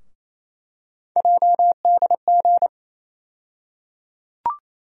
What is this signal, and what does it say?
Morse code, heard as a keyed tone, about 28 wpm, 700 Hz. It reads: JDG
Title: USA Call Sign Postfixes - 3 Letters 28wpm